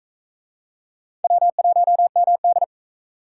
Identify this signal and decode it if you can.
Morse code, heard as a keyed tone, about 42 wpm, 700 Hz. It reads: W1MD